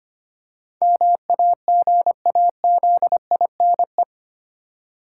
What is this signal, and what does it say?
Morse code, heard as a keyed tone, about 25 wpm, 690 Hz. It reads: MAGAZINE
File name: practice / 25wpm / Easy Words with Hints - 25wpm